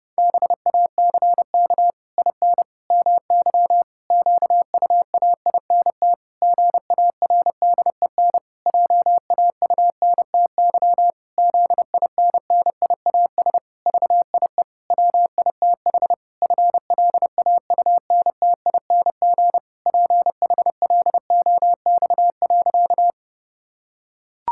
Morse code, heard as a keyed tone, about 30 wpm, 700 Hz. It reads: BACK IN MY QUAINT GARDEN JAUNTY ZINNIAS VIE WITH FLAUNTING PHLOX.
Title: Pangram Challenge v2 - 30wpm